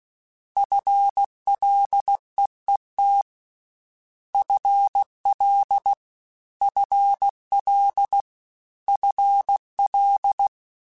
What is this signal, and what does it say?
Morse code, heard as a keyed tone, about 16 wpm, 800 Hz. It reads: FLEET FL FL FL
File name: Combinations / Combinations-mp3s / Combinations@15